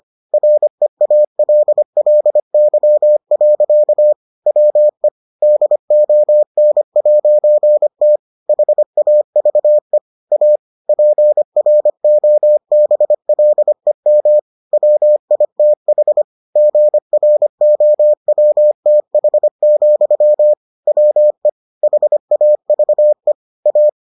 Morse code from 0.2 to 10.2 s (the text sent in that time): REALLY. WE DON'T HAVE